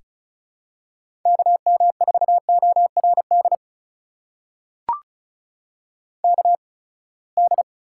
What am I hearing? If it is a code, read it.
Morse code, heard as a keyed tone, about 36 wpm, spaced as for 24 wpm, 700 Hz. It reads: KM4ORD KD
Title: USA Call Signs - 6 Characters - 8x Character Spacing - 35wpm